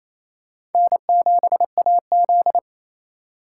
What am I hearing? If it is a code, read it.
Morse code, heard as a keyed tone, about 28 wpm, 700 Hz. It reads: N7AZ
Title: K1USN SST QSO Practice - 28wpm